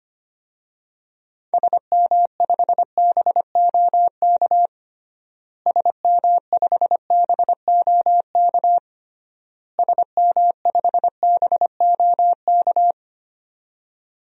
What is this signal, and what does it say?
Morse code, heard as a keyed tone, about 25 wpm, 700 Hz. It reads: SM5BOK SM5BOK SM5BOK
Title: DXCC-100 Call Signs - 6 Characters - Repeated 3x - 25wpm